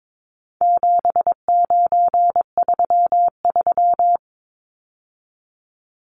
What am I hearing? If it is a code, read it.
Morse code, heard as a keyed tone, about 22 wpm, 700 Hz. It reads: 7933
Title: Numbers - 4-Digit 22wpm